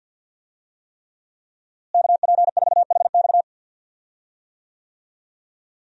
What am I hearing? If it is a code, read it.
Morse code, heard as a keyed tone, about 51 wpm, 700 Hz. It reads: KP4HX
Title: USA Call Signs - 5 Characters - Repeated 3x - 50wpm